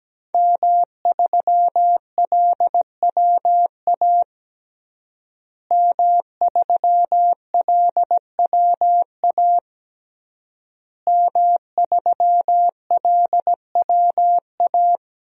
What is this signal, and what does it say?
Morse code, heard as a keyed tone, about 17 wpm, 700 Hz. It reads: M3LWA M3LWA M3LWA
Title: DXCC-100 Call Signs - 5 Characters - Repeated 3x - 17wpm